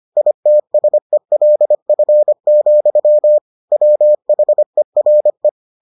Morse code, heard as a keyed tone, about 25 wpm, 600 Hz. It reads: ITSELF, WHERE